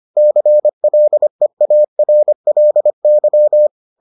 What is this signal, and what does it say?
Morse code, heard as a keyed tone, about 25 wpm, 600 Hz. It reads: CLEARLY